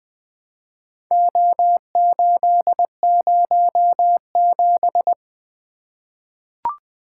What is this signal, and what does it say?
Morse code, heard as a keyed tone, about 21 wpm, 700 Hz. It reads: O807
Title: Four Letter-Number Combinations - 20wpm